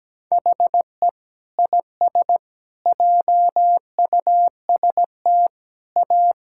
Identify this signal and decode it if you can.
Morse code, heard as a keyed tone, about 17 wpm, 700 Hz. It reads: HE IS JUST A
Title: Sentences from Top 1000 Words - Review Part 6 - 17wpm